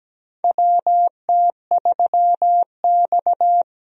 Morse code, heard as a keyed tone, about 17 wpm, 700 Hz. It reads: WT3X
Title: USA Call Signs - 4 Characters - Repeated 2x - 17wpm